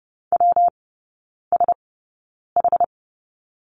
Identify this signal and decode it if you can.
Morse code, heard as a keyed tone, about 30 wpm, spaced as for 9 wpm, 700 Hz. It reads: WSH
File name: Lesson 19e - Letter w - Three Characters 30wpm